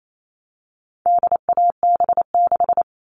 Morse code, heard as a keyed tone, about 28 wpm, 700 Hz. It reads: DAB6
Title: All Cut Numbers - 4-Digit 28wpm